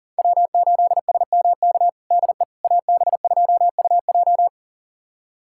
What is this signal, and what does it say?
Morse code, heard as a keyed tone, about 40 wpm, 690 Hz. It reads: W8SMK DE AB2UJ
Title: USA Callsign DE Callsign - 5 Characters - 40wpm